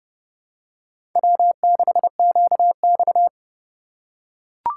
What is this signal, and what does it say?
Morse code, heard as a keyed tone, about 30 wpm, 700 Hz. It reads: W6QX
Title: USA Call Signs - 4 Characters - 3x Character Spacing - 30wpm